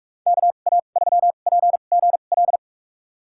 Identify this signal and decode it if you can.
Morse code, heard as a keyed tone, about 46 wpm, spaced as for 32 wpm, 700 Hz. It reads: KA3PGL
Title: USA Call Signs - 6 Characters - 2x Character Spacing - 45wpm